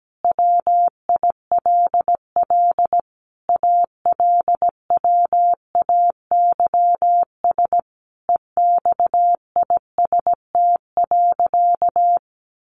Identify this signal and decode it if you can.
Morse code, heard as a keyed tone, about 17 wpm, 700 Hz. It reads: WILL ALWAYS EXIST.
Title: Sentences from Top 1500 Words 17wpm